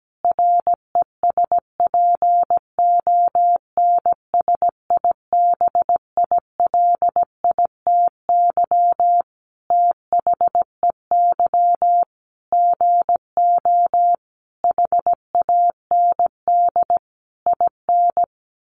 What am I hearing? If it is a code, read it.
Morse code, heard as a keyed tone, about 17 wpm, 700 Hz. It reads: RESPONSIBILITY THEY GO HAND IN
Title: Sentences from Top 1200 Words 17wpm